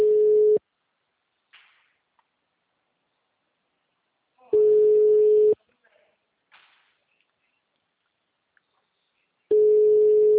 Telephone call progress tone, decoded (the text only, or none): ringback tone